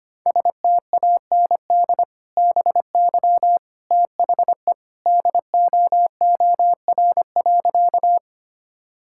Morse code, heard as a keyed tone, about 25 wpm, 700 Hz. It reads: STAND BY THE DOOR.
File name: practice / 25wpm / Sentences with 2-Word Phrases - 25wpm